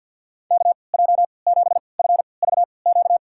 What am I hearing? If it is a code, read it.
Morse code, heard as a keyed tone, about 46 wpm, spaced as for 28 wpm, 700 Hz. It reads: KJ7FVQ
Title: USA Call Signs - 3x Character Spacing - 50wpm